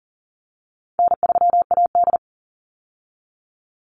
Morse code, heard as a keyed tone, about 40 wpm, 700 Hz. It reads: N3AD
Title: Easy Call Signs - 40wpm